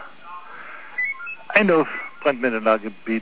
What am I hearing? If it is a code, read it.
Five-tone selective call (ZVEI), heard as quick runs of five tones, tones of about 80 ms, with speech. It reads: 89144